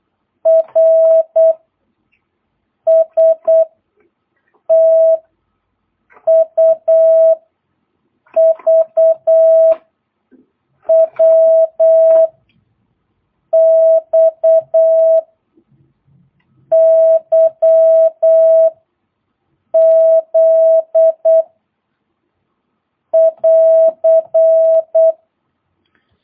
Morse code, heard as a keyed tone, about 8 wpm, 650 Hz. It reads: R S T U V W X Y Z <AR>
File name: jh8cbh / 2022cw